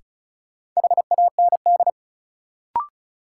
Morse code, heard as a keyed tone, about 36 wpm, 700 Hz. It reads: HAND
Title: Sentences from Top 200 Words - New Word Drill - 35wpm